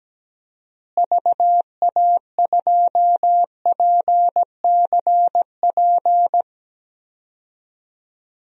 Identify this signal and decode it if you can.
Morse code, heard as a keyed tone, about 17 wpm, 700 Hz. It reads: VA2PCP
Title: DXCC-100 Call Signs - 6 Characters - Repeated 3x - 17wpm